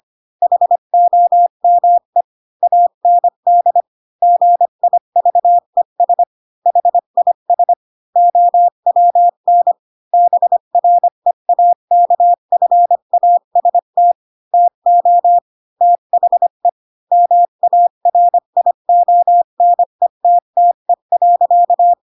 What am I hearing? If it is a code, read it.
Morse code, heard as a keyed tone, about 25 wpm, 700 Hz. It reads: HOME AND GIVES HIS OWN BREAKFAST TO THE MARIONETTE.
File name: Pinocchio / Pinocchio ch7 20 25 ik1hge